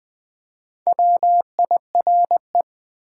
Morse code, heard as a keyed tone, about 20 wpm, 700 Hz. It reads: WIRE
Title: Top 180 QSO Elements 20wpm